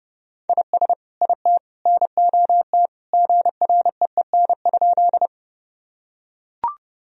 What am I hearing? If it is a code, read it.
Morse code, heard as a keyed tone, about 32 wpm, 700 Hz. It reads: IS IT NOT GREEN?
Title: Sentences from Top 1000 Words - Review Part 2 - 30wpm